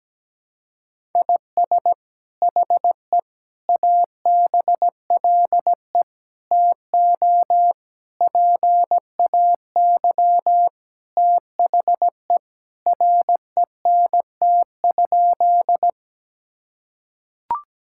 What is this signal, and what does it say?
Morse code, heard as a keyed tone, about 17 wpm, 700 Hz. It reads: IS HE ABLE TO PAY THE RENT?
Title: Sentences from Top 1000 Words - Review Part 5 - 17wpm